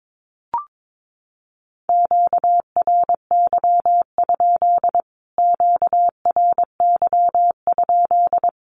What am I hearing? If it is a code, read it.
Morse code, heard as a keyed tone, about 22 wpm, 700 Hz. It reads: QRY? QRY?
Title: Q-Codes Repeated 3x 22wpm